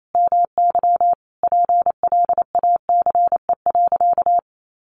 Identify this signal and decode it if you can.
Morse code, heard as a keyed tone, about 28 wpm, 700 Hz. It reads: MY PLACE.
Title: Sentences from Top 1200 Words 28wpm